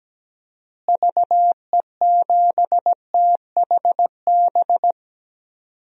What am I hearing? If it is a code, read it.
Morse code, heard as a keyed tone, about 17 wpm, 700 Hz. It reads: VE7THB
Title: DXCC-100 Call Signs - 6 Characters - Repeated 2x - 17wpm